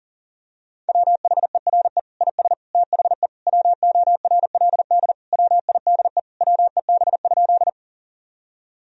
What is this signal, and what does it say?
Morse code, heard as a keyed tone, about 40 wpm, 700 Hz. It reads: WHERE IS THE WORLD WIDE WEB?
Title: Sentences from Top 1000 Words - Review Part 4 - 40wpm